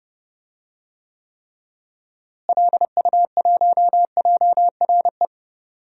Morse code, heard as a keyed tone, about 32 wpm, 700 Hz. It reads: LU1JRE
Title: DXCC-100 Call Signs - 6 Characters - Repeated 3x - 30wpm